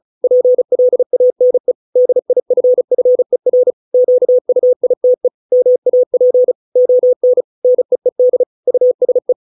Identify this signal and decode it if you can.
Morse code, heard as a keyed tone, about 34 wpm, 500 Hz. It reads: PLANE DIFFER QUITE MAP ON NEED USE